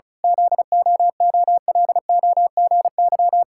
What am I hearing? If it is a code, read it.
Morse code, heard as a keyed tone, about 36 wpm, 700 Hz. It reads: ZOOLOGY
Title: Pangram Challenge v2 - 35wpm